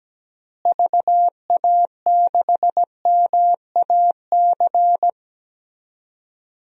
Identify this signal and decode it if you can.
Morse code, heard as a keyed tone, about 17 wpm, 700 Hz. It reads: VA6MAC
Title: DXCC-100 Call Signs - 6 Characters - Repeated 3x - 17wpm